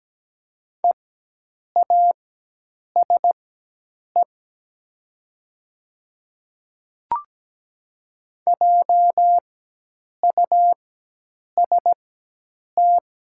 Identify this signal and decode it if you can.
Morse code, heard as a keyed tone, about 17 wpm, spaced as for 8 wpm, 700 Hz. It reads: EASE JUST
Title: 4x Character Spacing - Top 300 Words - 17wpm